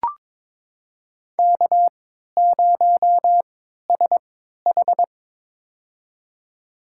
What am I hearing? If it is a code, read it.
Morse code, heard as a keyed tone, about 23 wpm, spaced as for 13 wpm, 700 Hz. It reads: K0SH
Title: USA Call Signs - 4 Characters - 3x Character Spacing - 22wpm